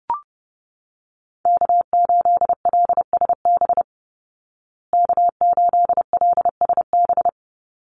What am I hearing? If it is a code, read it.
Morse code, heard as a keyed tone, about 30 wpm, 700 Hz. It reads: K8LSB K8LSB
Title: Call Signs Repeated 3x - 30wpm